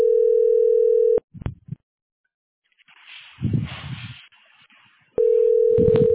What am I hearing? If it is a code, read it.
Ringback tone (North America), 440 + 480 Hz.